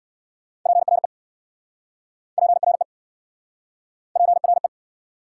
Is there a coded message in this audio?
Yes: Morse code from 0.7 to 4.7 s, heard as a keyed tone, about 69 wpm, 700 Hz. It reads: PRE PRE PRE